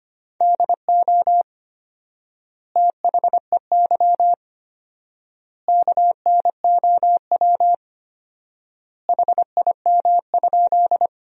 Morse code, heard as a keyed tone, about 25 wpm, 700 Hz. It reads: DO THEY KNOW HIM?
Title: 4x Word-Spacing - Sentences from Top 100 Words - 25wpm